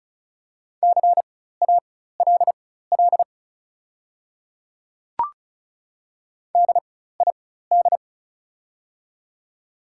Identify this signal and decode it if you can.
Morse code, heard as a keyed tone, about 34 wpm, spaced as for 16 wpm, 700 Hz. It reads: CALL DID
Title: 4x Character Spacing - Top 200 Words - 35wpm